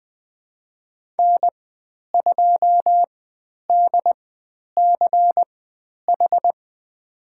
Morse code, heard as a keyed tone, about 20 wpm, spaced as for 10 wpm, 700 Hz. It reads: N2DCH
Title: 095 Lesson 16g - Letter c - Call Signs 20wpm